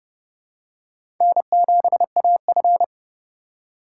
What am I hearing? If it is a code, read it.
Morse code, heard as a keyed tone, about 30 wpm, 700 Hz. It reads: N7AF
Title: USA Call Signs - 4 Characters - 2x Character Spacing - 30wpm